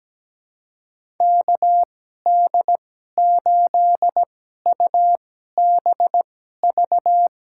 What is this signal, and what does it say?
Morse code, heard as a keyed tone, about 17 wpm, spaced as for 12 wpm, 700 Hz. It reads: KD8UBV
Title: USA Call Signs - 2x Character Spacing - 17wpm